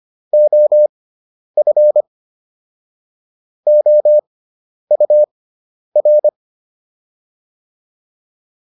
Morse code, heard as a keyed tone, about 25 wpm, spaced as for 10 wpm, 600 Hz. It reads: OF OUR